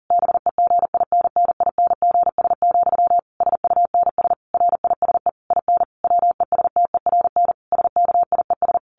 Morse code, heard as a keyed tone, about 40 wpm, 700 Hz. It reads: BEGINNINGS, SUNS RISE IN WESTERN SKIES
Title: A Sound Of Thunder 40wpm